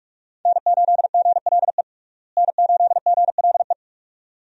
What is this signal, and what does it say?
Morse code, heard as a keyed tone, about 46 wpm, 700 Hz. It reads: N8GLE N8GLE
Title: USA Call Signs - 5 Characters - Repeated 2x - 45wpm